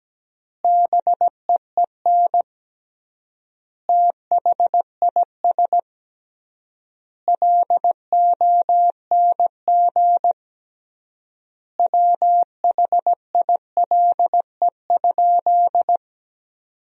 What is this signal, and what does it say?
Morse code, heard as a keyed tone, about 17 wpm, 700 Hz. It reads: BEEN THIS LONG WHILE?
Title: 3x Word-Spacing - Sentences from Top 300 Words - 17wpm